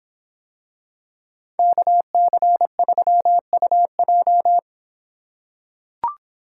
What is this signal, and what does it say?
Morse code, heard as a keyed tone, about 26 wpm, 700 Hz. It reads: KC3UJ